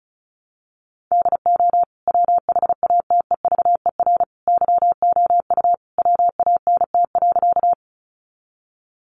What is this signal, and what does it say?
Morse code, heard as a keyed tone, about 34 wpm, 700 Hz. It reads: DO WHATEVER YOU WANT.
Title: Sentences from Top 1100 Words 35wpm